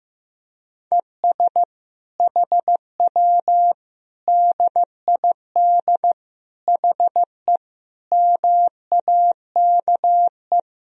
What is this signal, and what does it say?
Morse code, heard as a keyed tone, about 15 wpm, 700 Hz. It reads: ES HW DID HE MAKE
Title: Sentences from Top 400 Words and Abbreviations 15wpm